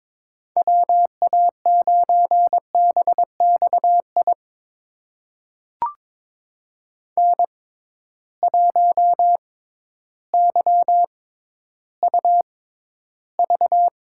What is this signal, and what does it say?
Morse code, heard as a keyed tone, about 22 wpm, spaced as for 12 wpm, 700 Hz. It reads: WA9BXI N1YUV
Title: USA Call Signs - 6x Character Spacing - 22wpm